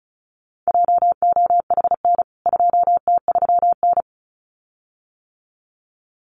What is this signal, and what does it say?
Morse code, heard as a keyed tone, about 34 wpm, 700 Hz. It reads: JOHN 2T3N
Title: CWT Exchange 35wpm